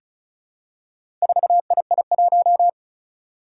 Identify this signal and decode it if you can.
Morse code, heard as a keyed tone, about 36 wpm, 700 Hz. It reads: 4II1
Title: Four Challenging Letter-Number Combinations 35wpm